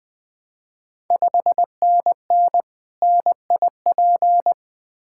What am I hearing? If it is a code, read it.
Morse code, heard as a keyed tone, about 20 wpm, 700 Hz. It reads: 5NN NIP
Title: Ontario QSO Party Exchange 20wpm